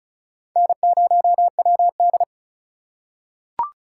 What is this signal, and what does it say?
Morse code, heard as a keyed tone, about 34 wpm, 700 Hz. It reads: N0WD
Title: USA Call Signs - 4 Characters - 4x Character Spacing - 35wpm